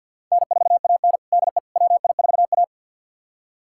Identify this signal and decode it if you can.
Morse code, heard as a keyed tone, about 52 wpm, 700 Hz. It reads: N4AN DE WI4A
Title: USA Callsign DE Callsign - 4 Characters - 50wpm